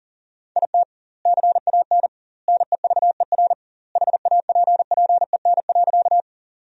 Morse code, heard as a keyed tone, about 40 wpm, 700 Hz. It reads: IT CAN NEVER HAPPEN.
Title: 2x Word-Spacing - Sentences from Top 400 Words - 40wpm